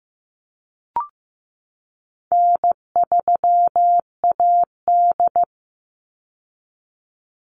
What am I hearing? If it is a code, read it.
Morse code, heard as a keyed tone, about 15 wpm, 700 Hz. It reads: N3AD
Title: Call Signs Repeated 3x - 15wpm